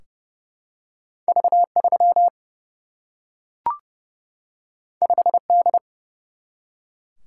Morse code, heard as a keyed tone, about 28 wpm, 700 Hz. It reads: V 3 5 D
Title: Single-Letter Single-Number 30wpm